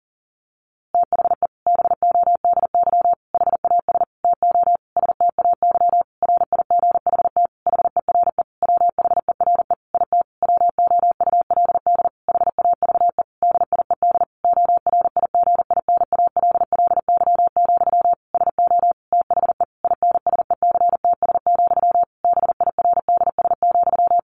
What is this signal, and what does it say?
Morse code, heard as a keyed tone, about 40 wpm, 700 Hz. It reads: THE BODY HAS TO STAY RIGHT HERE WHERE IT WOULD HAVE DIED ORIGINALLY, SO THE INSECTS, BIRDS,
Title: A Sound Of Thunder 40wpm